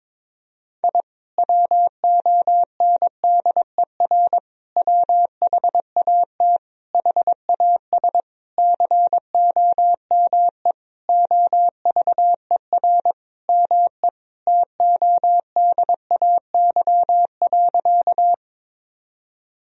Morse code, heard as a keyed tone, about 22 wpm, 700 Hz. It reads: I WONDER WHAT HAS COME OVER ME TODAY.
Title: Sentences from Top 1000 Words - Review Part 4 - 22wpm